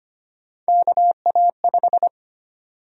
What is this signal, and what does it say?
Morse code, heard as a keyed tone, about 25 wpm, 700 Hz. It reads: KA5
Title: USA Call Sign Prefixes - 25wpm